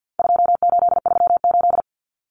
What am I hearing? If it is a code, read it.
Morse code, heard as a keyed tone, about 48 wpm, 700 Hz. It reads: 2838
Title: Common Cut Numbers - 4-Digit 50wpm